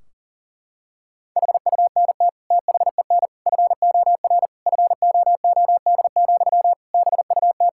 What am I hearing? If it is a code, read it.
Morse code, heard as a keyed tone, about 40 wpm, 700 Hz. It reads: HUNT THEN FOR FOOD, BUT
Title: Sentences from Top 1600 Words 40wpm